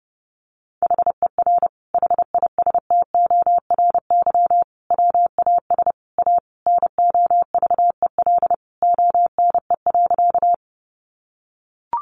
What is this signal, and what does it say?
Morse code, heard as a keyed tone, about 30 wpm, 700 Hz. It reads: HER HISTORY WAS A NOVEL ONE.
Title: Sentences from Top 1300 Words 30wpm